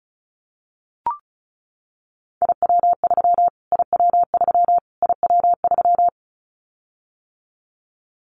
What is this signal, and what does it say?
Morse code, heard as a keyed tone, about 34 wpm, 700 Hz. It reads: IW3 IW3 IW3
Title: Callsign Prefixes for Top 25 DXCC - Repeated 3x - 35wpm